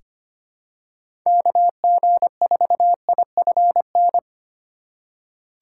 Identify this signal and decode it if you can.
Morse code, heard as a keyed tone, about 25 wpm, 700 Hz. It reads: KG4IFN
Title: USA Call Signs - 3x Character Spacing - 25wpm